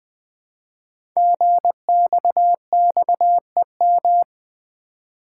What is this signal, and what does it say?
Morse code, heard as a keyed tone, about 20 wpm, 700 Hz. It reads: GXXEM
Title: Code Groups - 5 Letters - 20wpm